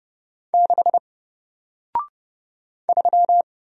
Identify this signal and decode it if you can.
Morse code, heard as a keyed tone, about 30 wpm, 700 Hz. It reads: 6 3
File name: Single-Letter-Number-30wpm